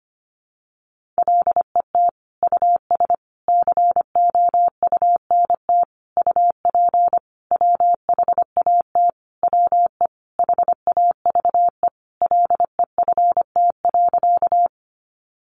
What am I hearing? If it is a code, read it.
Morse code, heard as a keyed tone, about 25 wpm, 700 Hz. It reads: LET US COUNT UP WHAT WE HAVE LEFT.
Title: Sentences from Top 1500 Words 25wpm